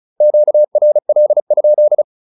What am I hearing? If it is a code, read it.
Morse code, heard as a keyed tone, about 36 wpm, 600 Hz. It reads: QRL?